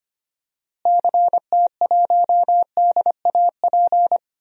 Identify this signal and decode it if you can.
Morse code, heard as a keyed tone, about 25 wpm, 700 Hz. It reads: CT1DAP
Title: DXCC-100 Call Signs - 6 Characters - Repeated 3x - 25wpm